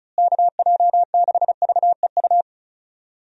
Morse code, heard as a keyed tone, about 34 wpm, 700 Hz. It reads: KJ6VEU
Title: USA Call Signs - 6 Characters - Repeated 3x - 35wpm